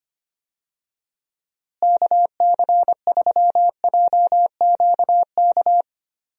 Morse code, heard as a keyed tone, about 25 wpm, 700 Hz. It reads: KC3JQK
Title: USA Call Signs - 6 Characters - Repeated 3x - 25wpm